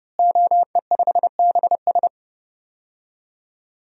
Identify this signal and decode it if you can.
Morse code, heard as a keyed tone, about 32 wpm, 700 Hz. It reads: OE5BS
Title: DXCC-100 Call Signs - 5 Characters - Repeated 3x - 30wpm